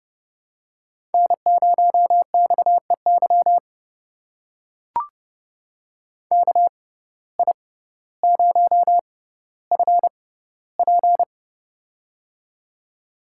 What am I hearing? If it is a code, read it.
Morse code, heard as a keyed tone, about 29 wpm, spaced as for 15 wpm, 700 Hz. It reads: N0XEY KI0FP
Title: USA Call Signs - 5 Characters - 6x Character Spacing - 30wpm